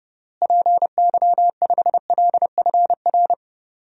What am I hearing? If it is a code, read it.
Morse code, heard as a keyed tone, about 31 wpm, 700 Hz. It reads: PY5LFR
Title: DXCC-100 Call Signs - 6 Characters - Repeated 2x - 30wpm